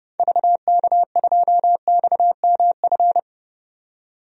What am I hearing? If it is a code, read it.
Morse code, heard as a keyed tone, about 30 wpm, 700 Hz. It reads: VK2XMF
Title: DXCC-100 Call Signs - 6 Characters - 30wpm